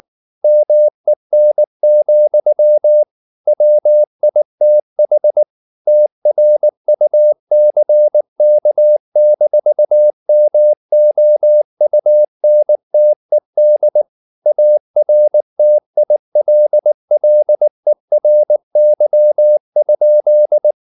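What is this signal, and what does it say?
Morse code, heard as a keyed tone, about 19 wpm, 600 Hz. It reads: MEN, WITH TRUCK-MOUNTED ARTILLERY?
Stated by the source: Morse code at 19 wpm from CW med Gnister, reading Ken Follett's novel Never in English.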